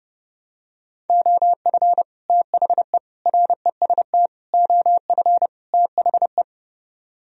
Morse code, heard as a keyed tone, about 30 wpm, 700 Hz. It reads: OF THE REST OF THE